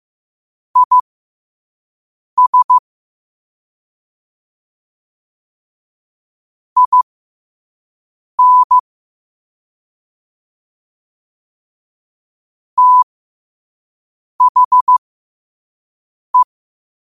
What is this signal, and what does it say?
Morse code, heard as a keyed tone, about 15 wpm, spaced as for 5 wpm, 1000 Hz. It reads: IS IN THE
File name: qotdfiles / 5wpm